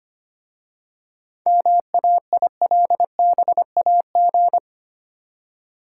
Morse code, heard as a keyed tone, about 25 wpm, 690 Hz. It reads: MAILBAG
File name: practice / 25wpm / Easy Words with Hints - 25wpm